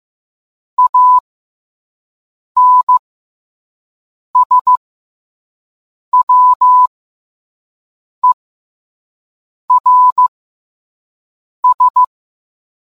Morse code, heard as a keyed tone, about 15 wpm, spaced as for 5 wpm, 1000 Hz. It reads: ANSWERS